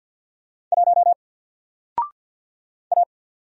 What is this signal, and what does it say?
Morse code, heard as a keyed tone, about 48 wpm, 700 Hz. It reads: 1 A
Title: Single Letter-Number 50wpm